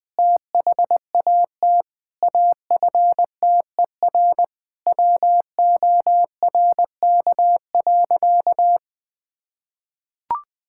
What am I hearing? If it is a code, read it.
Morse code, heard as a keyed tone, about 20 wpm, 700 Hz. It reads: THAT AFTER WORK.
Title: Sentences from Top 1000 Words - Review Part 5 - 20wpm